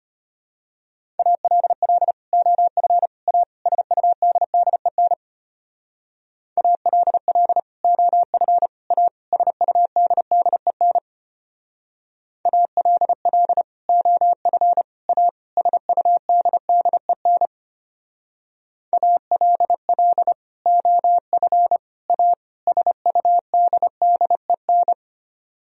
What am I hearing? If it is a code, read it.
Morse code, heard as a keyed tone, about 32 wpm, 700 Hz. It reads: ALL OF A SUDDEN ALL OF A SUDDEN ALL OF A SUDDEN ALL OF A SUDDEN